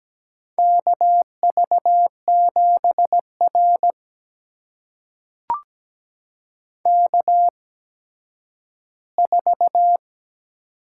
Morse code, heard as a keyed tone, about 17 wpm, spaced as for 10 wpm, 700 Hz. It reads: KV7R K4
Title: USA Call Signs - 4 Characters - 8x Character Spacing - 17wpm